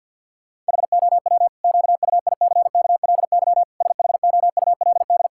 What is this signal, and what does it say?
Morse code, heard as a keyed tone, about 50 wpm, 700 Hz. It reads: HOW QUICKLY SHOULD
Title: Sentences from Top 1000 Words - Review Part 5 - 50wpm